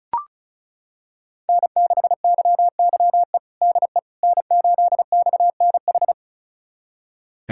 Morse code, heard as a keyed tone, about 34 wpm, 700 Hz. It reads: N6YYE DE N8XNH
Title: USA Callsign DE Callsign - 5 Characters - 35wpm